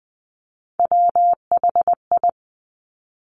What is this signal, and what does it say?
Morse code, heard as a keyed tone, about 20 wpm, 700 Hz. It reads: WHI
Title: Top Three Letter Combinations in Words 20wpm